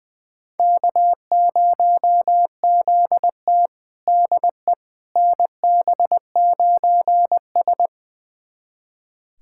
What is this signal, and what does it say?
Morse code, heard as a keyed tone, about 20 wpm, 700 Hz. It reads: K0ZT DE NB9S
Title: USA Callsign DE Callsign - 4 Characters - 20wpm